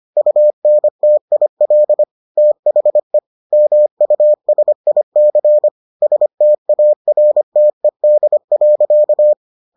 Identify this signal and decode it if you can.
Morse code, heard as a keyed tone, about 25 wpm, 600 Hz. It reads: UNTIL THE MUSIC STARTED.